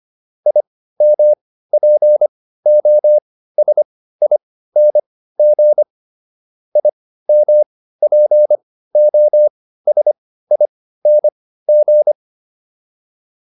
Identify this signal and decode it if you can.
Morse code, heard as a keyed tone, about 25 wpm, spaced as for 15 wpm, 600 Hz. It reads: IMPOSING IMPOSING